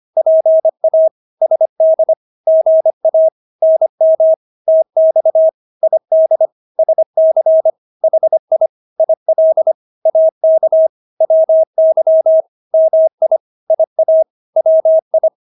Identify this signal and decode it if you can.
Morse code, heard as a keyed tone, about 25 wpm, 640 Hz. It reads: PA SD GA NM TX ID SC HI IL AK WY MI IA WI
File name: State-Abbreviations-25